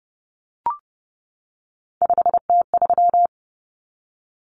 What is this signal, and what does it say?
Morse code, heard as a keyed tone, about 30 wpm, 700 Hz. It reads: 5T3
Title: Common Cut Numbers - 3-Digit 30wpm